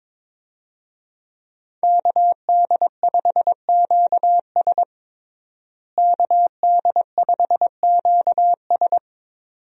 Morse code, heard as a keyed tone, about 22 wpm, 700 Hz. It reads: KD5QS KD5QS
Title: USA Call Signs - 5 Characters - Repeated 2x - 22wpm